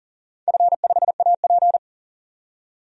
Morse code, heard as a keyed tone, about 40 wpm, 700 Hz. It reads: F5AP
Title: DXCC-100 Call Signs - 4 Characters - 40wpm